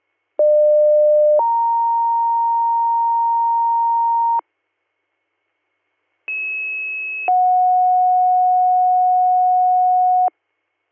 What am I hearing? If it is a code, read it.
Two-tone page, heard as a one-second tone, then a three-second tone, about 600 then 930 Hz, about 2600 then 730 Hz. It is heard from 0.4 to 10.3 s.